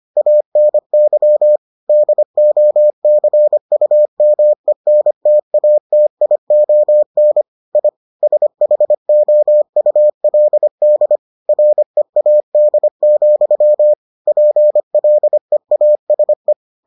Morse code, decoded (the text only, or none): ANY DOCUMENTATION I SHOULD READ, PLEASE